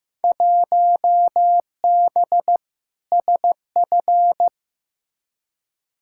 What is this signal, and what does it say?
Morse code, heard as a keyed tone, about 15 wpm, 700 Hz. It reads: 1B SF
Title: Field Day Exchanges Encore - 15wpm